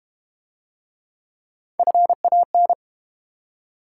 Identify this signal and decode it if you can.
Morse code, heard as a keyed tone, about 32 wpm, 700 Hz. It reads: FAN